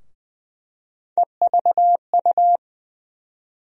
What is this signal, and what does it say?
Morse code, heard as a keyed tone, about 20 wpm, 700 Hz. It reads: EVU
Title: USA Call Sign Postfixes - 3 Letters 20wpm